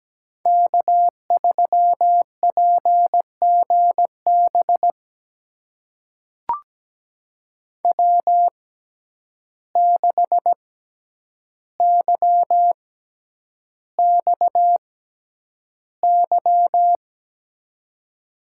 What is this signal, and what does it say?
Morse code, heard as a keyed tone, about 17 wpm, spaced as for 9 wpm, 700 Hz. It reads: K3PGB W6YXY
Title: USA Call Signs - 5 Characters - 6x Character Spacing - 17wpm